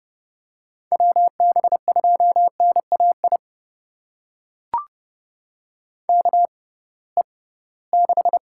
Morse code, heard as a keyed tone, about 30 wpm, spaced as for 19 wpm, 700 Hz. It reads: WB2NAI KE6
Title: USA Call Signs - 6 Characters - 6x Character Spacing - 30wpm